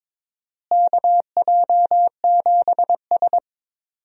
Morse code, heard as a keyed tone, about 22 wpm, 700 Hz. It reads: KJ7S
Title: USA Call Signs - 4 Characters - Repeated 2x - 22wpm